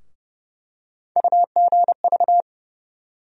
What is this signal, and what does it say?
Morse code, heard as a keyed tone, about 30 wpm, 700 Hz. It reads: UGV
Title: USA Call Sign Postfixes - 3 Letters 30wpm